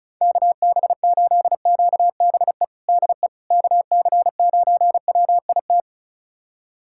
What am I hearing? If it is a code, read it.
Morse code, heard as a keyed tone, about 34 wpm, 700 Hz. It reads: KB8QBE DE KC9WIT
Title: USA Callsign DE Callsign 35wpm